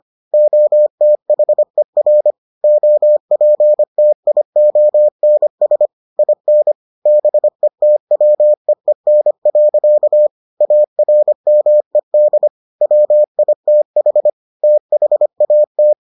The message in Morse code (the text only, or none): OTHER OPTIONS IN BETWEEN. ARMED WITH THAT